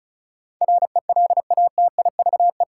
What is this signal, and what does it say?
Morse code, heard as a keyed tone, about 36 wpm, 700 Hz. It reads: RELATIVE
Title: Sentences from Top 1400 Words - New Word Drill - 35wpm